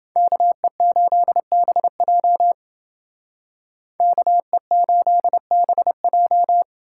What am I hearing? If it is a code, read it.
Morse code, heard as a keyed tone, about 28 wpm, 700 Hz. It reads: KE8BJ KE8BJ